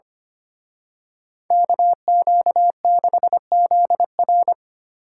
Morse code, heard as a keyed tone, about 30 wpm, 700 Hz. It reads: KQ6ZR
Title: Speed Racing - USA Call Signs 25wpm